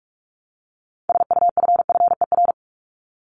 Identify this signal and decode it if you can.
Morse code, heard as a keyed tone, about 44 wpm, 700 Hz. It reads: SUFFER
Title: ING Suffix - Top 100 Words with Base Words 45wpm